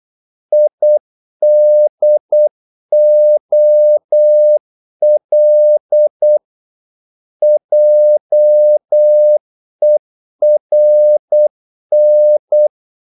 Morse code, heard as a keyed tone, about 8 wpm, 600 Hz. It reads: IDOL JERN
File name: Korte ord DK 8wpm